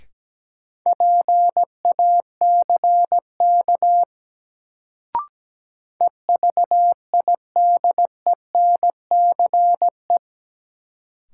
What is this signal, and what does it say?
Morse code, heard as a keyed tone, about 17 wpm, 700 Hz. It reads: PACK EVIDENCE
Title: Sentences from Top 1300 Words - New Word Drill Alt - 17wpm